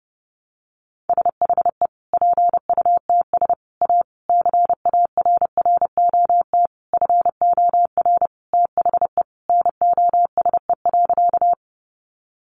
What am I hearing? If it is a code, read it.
Morse code, heard as a keyed tone, about 28 wpm, 700 Hz. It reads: SHE PUTS A CARROT FOR THE NOSE.
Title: Easy Phrases 30wpm